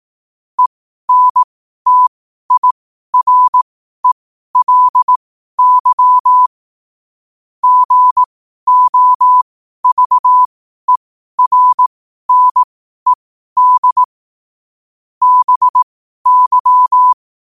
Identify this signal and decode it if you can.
Morse code, heard as a keyed tone, about 18 wpm, spaced as for 12 wpm, 1000 Hz. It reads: ENTIRELY GOVERNED BY
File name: qotdfiles / 12wpm